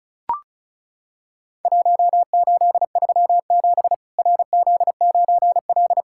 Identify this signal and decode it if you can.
Morse code, heard as a keyed tone, about 34 wpm, 700 Hz. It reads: 1837 RZ9L
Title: Worked All Europe DX Contest - QTC Exchange - 35wpm